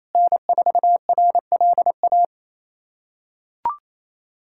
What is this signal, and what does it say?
Morse code, heard as a keyed tone, about 28 wpm, 700 Hz. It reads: N4RLA
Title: USA Callsign DE Callsign - 5 Characters - 28wpm